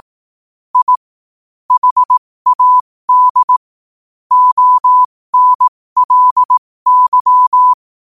Morse code, heard as a keyed tone, about 18 wpm, spaced as for 15 wpm, 1000 Hz. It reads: I HAD ONLY
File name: qotdfiles / 15wpm